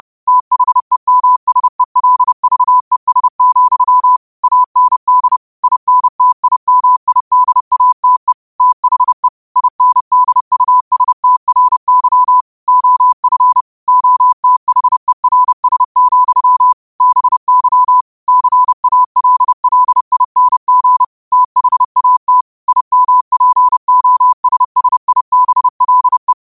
Morse code, heard as a keyed tone, about 30 wpm, 1000 Hz. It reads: THEMSELVES, AND INTIMIDATE THE INDUSTRY OF OTHERS, BY CALLING THAT IMPOSSIBLE
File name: qotdfiles / 30wpm